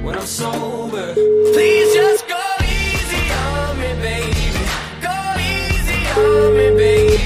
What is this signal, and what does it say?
Ringback tone (Europe, 425 Hz), with noise.